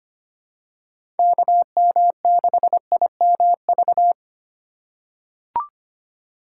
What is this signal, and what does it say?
Morse code, heard as a keyed tone, about 25 wpm, 700 Hz. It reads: KM6IMV
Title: USA Call Signs - 6 Characters - 4x Character Spacing - 25wpm